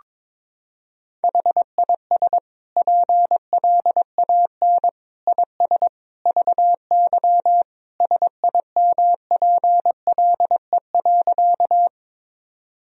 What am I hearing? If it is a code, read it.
Morse code, heard as a keyed tone, about 21 wpm, 700 Hz. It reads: HIS PLAN IS VY SIMPLE.
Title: Sentences from Top 500 Words and Abbreviations 22wpm